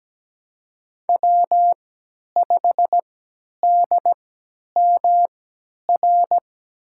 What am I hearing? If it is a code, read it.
Morse code, heard as a keyed tone, about 17 wpm, spaced as for 10 wpm, 700 Hz. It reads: W5DMR
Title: USA Call Signs - 5 Characters - 3x Character Spacing - 17wpm